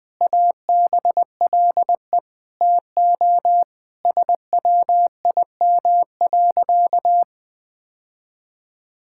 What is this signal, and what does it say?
Morse code, heard as a keyed tone, about 20 wpm, 700 Hz. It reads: ABLE TO SWIM.
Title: Sentences with 5-Word Phrases - Spoken Beginning - 20wpm